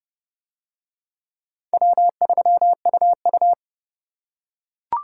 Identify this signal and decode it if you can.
Morse code, heard as a keyed tone, about 29 wpm, 700 Hz. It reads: W3UU